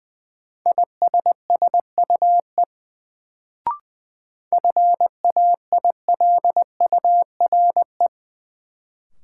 Morse code, heard as a keyed tone, about 20 wpm, 700 Hz. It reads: ISSUE FAILURE
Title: Sentences from Top 900 Words - New Word Drill - 20wpm